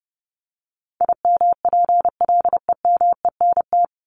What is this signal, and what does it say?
Morse code, heard as a keyed tone, about 30 wpm, 700 Hz. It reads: IMPLEMENT
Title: Top 1501-1750 Words 30wpm